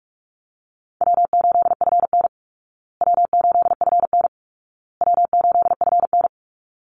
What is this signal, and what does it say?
Morse code, heard as a keyed tone, about 44 wpm, 700 Hz. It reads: W8FN W8FN W8FN
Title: Easy Call Signs Repeated 3x - 45wpm